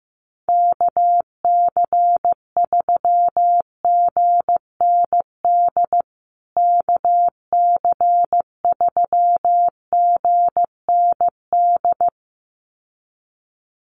Morse code, heard as a keyed tone, about 15 wpm, 700 Hz. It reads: KC3GND KC3GND
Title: USA Call Signs Repeated 2x 15wpm